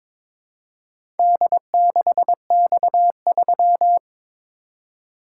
Morse code, heard as a keyed tone, about 22 wpm, 700 Hz. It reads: D6X3
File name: practice / 22wpm / Four Letter-Number Combinations - 22wpm